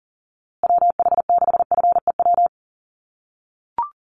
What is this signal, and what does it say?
Morse code, heard as a keyed tone, about 40 wpm, 700 Hz. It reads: WH6FEW
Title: USA Call Signs Repeated 3x 40wpm